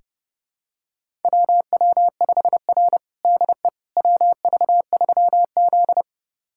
Morse code, heard as a keyed tone, about 32 wpm, 700 Hz. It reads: WW5R DE WV3Z
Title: USA Callsign DE Callsign - 4 Characters - 30wpm